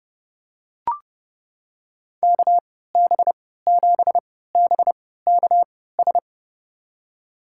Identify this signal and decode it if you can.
Morse code, heard as a keyed tone, about 30 wpm, spaced as for 17 wpm, 700 Hz. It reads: KB7BKS
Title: USA Call Signs - 6 Characters - 3x Character Spacing - 30wpm